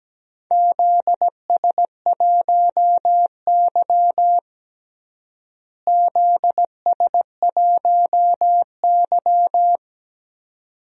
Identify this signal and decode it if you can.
Morse code, heard as a keyed tone, about 17 wpm, 700 Hz. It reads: ZS1Y ZS1Y
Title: DXCC-100 Call Signs - 4 Characters - Repeated 2x - 17wpm